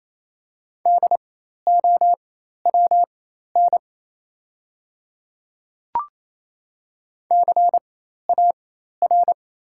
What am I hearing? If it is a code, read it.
Morse code, heard as a keyed tone, about 28 wpm, spaced as for 13 wpm, 700 Hz. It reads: DOWN CAR
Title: 4x Character Spacing - Top 300 Words - 28wpm